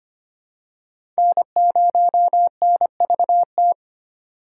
Morse code, heard as a keyed tone, about 25 wpm, 700 Hz. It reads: N0NVT
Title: USA Call Signs - 5 Characters - 3x Character Spacing - 25wpm